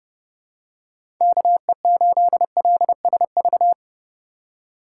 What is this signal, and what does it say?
Morse code, heard as a keyed tone, about 32 wpm, 700 Hz. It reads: KE8LSV
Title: USA Call Signs - 3x Character Spacing - 30wpm